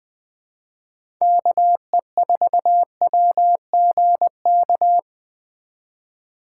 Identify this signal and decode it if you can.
Morse code, heard as a keyed tone, about 20 wpm, 700 Hz. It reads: KE4WGK
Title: USA Call Signs - 6 Characters - 4x Character Spacing - 20wpm